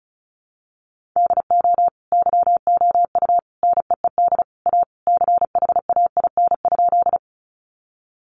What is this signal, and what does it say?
Morse code, heard as a keyed tone, about 34 wpm, 700 Hz. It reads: DO YOU NEED A CHAIN?
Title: Sentences from Top 1500 Words 35wpm